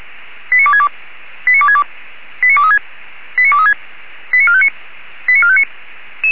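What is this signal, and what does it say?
Five-tone selective call (ZVEI), heard as quick runs of five tones, tones of about 70 ms. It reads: 89262 89262 89236 89236 89459 89459
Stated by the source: ZVEI five-tone selective call received on SDR.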